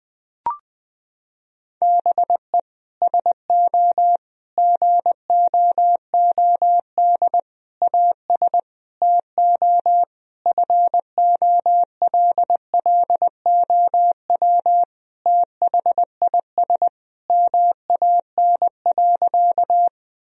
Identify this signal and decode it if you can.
Morse code, heard as a keyed tone, about 20 wpm, 700 Hz. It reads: BE SO GOOD AS TO FOLLOW THIS MAN.
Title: Sentences from Top 1000 Words - Review Part 1 - 20wpm